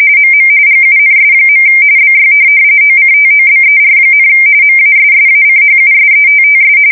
RTTY, about 45.45 baud. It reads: MM GUNS. BY OCTOBER, MASSACHUSETTS